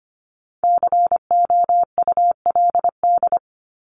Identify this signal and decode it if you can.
Morse code, heard as a keyed tone, about 25 wpm, 700 Hz. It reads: COULD
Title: Top 200 Words Repeated 25wpm